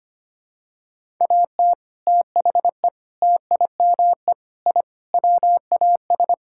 Morse code, heard as a keyed tone, about 25 wpm, 700 Hz. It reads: AT THE TIME I WAS
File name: practice / 25wpm / Sentences with 3-Word Phrases - 25wpm